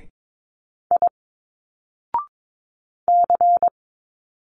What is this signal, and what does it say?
Morse code, heard as a keyed tone, about 22 wpm, 700 Hz. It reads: I C
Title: Single Letters 22wpm